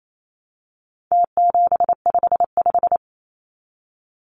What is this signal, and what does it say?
Morse code, heard as a keyed tone, about 28 wpm, 700 Hz. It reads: T755
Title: Numbers - Contest Style - 4-Digit 28wpm